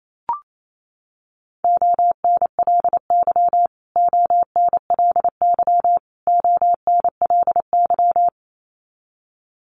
Morse code, heard as a keyed tone, about 28 wpm, 700 Hz. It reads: ONLY ONLY ONLY
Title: Top 500 Words Repeated 28wpm